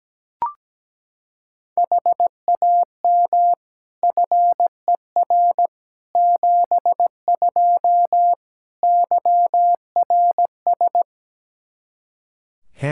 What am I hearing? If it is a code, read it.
Morse code, heard as a keyed tone, about 17 wpm, 700 Hz. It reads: HAM FER 72 YRS
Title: K1USN SST QSO Practice 17wpm